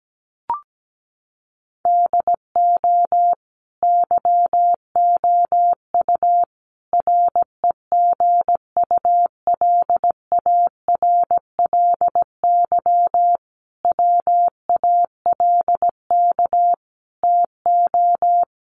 Morse code, heard as a keyed tone, about 17 wpm, 700 Hz. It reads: DO YOU REGULARLY WALK TO
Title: Sentences from Top 1400 Words 17wpm